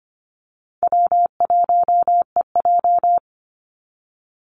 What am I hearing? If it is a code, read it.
Morse code, heard as a keyed tone, about 25 wpm, 700 Hz. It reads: W1EJ
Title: NA SOTA Activator Callsigns 25wpm